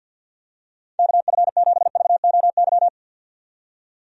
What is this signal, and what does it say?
Morse code, heard as a keyed tone, about 46 wpm, 700 Hz. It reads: KF7VOY
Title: USA Call Signs - 6 Characters - 50wpm